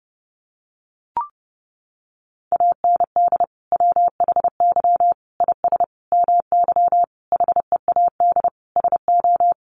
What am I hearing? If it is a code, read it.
Morse code, heard as a keyed tone, about 30 wpm, 700 Hz. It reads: AND WHY IS MY HEAD SO
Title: Sentences from Top 1100 Words 30wpm